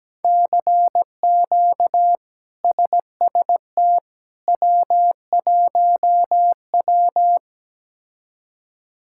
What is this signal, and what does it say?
Morse code, heard as a keyed tone, about 17 wpm, 700 Hz. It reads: CQ SST W1W
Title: K1USN SST QSO Practice - 17wpm